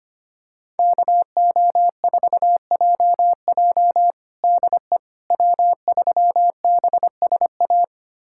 Morse code, heard as a keyed tone, about 25 wpm, 700 Hz. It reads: KO4JJ DE W3BSA
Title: USA Callsign DE Callsign - 5 Characters - 25wpm